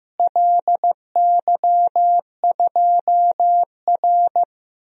Morse code, heard as a keyed tone, about 15 wpm, 700 Hz. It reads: LY2R
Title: DXCC-100 Call Signs - 4 Characters - Repeated 2x - 15wpm